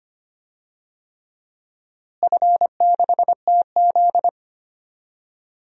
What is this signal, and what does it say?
Morse code, heard as a keyed tone, about 25 wpm, 700 Hz. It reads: F6TZ
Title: DXCC-100 Call Signs - 4 Characters - Repeated 3x - 25wpm